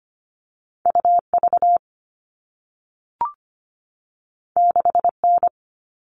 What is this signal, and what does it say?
Morse code, heard as a keyed tone, about 25 wpm, 700 Hz. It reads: U V 6 N
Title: All Cut Numbers - 2-Digit 25wpm